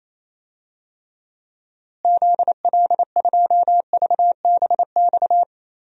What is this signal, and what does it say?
Morse code, heard as a keyed tone, about 28 wpm, 700 Hz. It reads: ZL2VBX
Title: DXCC-100 Call Signs - 6 Characters - Repeated 2x - 28wpm